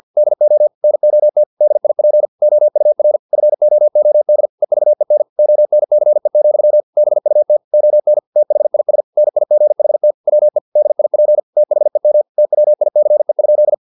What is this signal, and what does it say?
Morse code, heard as a keyed tone, about 50 wpm, 600 Hz. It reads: DO NOT DIP OUR FOOD EVEN ONCE, BUT ON THIS NIGHT WE DIP THEM TWICE?